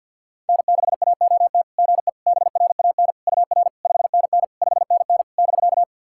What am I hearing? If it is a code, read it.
Morse code, heard as a keyed tone, about 50 wpm, 700 Hz. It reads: N6AOT GE BRAN UR 5NN 5NN <BK>
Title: SOTA QSO Practice 50wpm